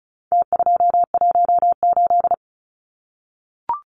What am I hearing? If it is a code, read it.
Morse code, heard as a keyed tone, about 34 wpm, 700 Hz. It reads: T218
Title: Numbers - Contest Style - 4-Digit 35wpm